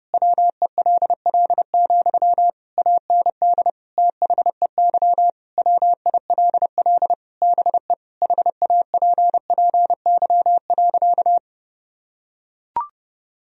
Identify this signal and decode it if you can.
Morse code, heard as a keyed tone, about 30 wpm, 700 Hz. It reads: WELL, AND THEY WILL BE HAPPY.
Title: Sentences from Top 1000 Words - Review Part 2 - 30wpm